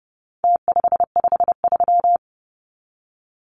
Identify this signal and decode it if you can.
Morse code, heard as a keyed tone, about 30 wpm, 700 Hz. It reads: T553
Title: Numbers - Contest Style - 4-Digit 30wpm